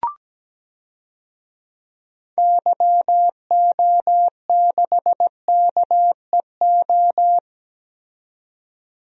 Morse code, heard as a keyed tone, about 17 wpm, 700 Hz. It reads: YO6KEO
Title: DXCC-100 Call Signs - 6 Characters - 17wpm